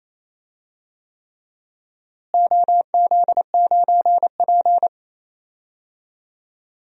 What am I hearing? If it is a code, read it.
Morse code, heard as a keyed tone, about 28 wpm, 700 Hz. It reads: OZ9P
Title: DXCC-100 Call Signs - 4 Characters - Repeated 3x - 28wpm